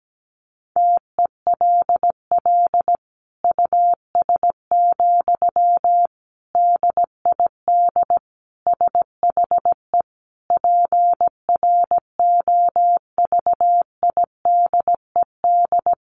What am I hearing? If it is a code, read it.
Morse code, heard as a keyed tone, about 17 wpm, 700 Hz. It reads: TELL US, DID SHE PROVIDED
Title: Sentences from Top 1500 Words 17wpm